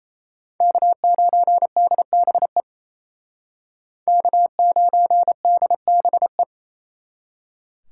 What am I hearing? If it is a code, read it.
Morse code, heard as a keyed tone, about 35 wpm, 700 Hz. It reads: K9DBE K9DBE
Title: Speed Racing - USA 1x3 Call Signs 28wpm